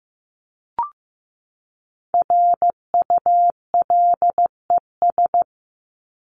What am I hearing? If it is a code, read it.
Morse code, heard as a keyed tone, about 15 wpm, 700 Hz. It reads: RULES
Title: ES:S Suffix - Top 100 Words with Base Words 15wpm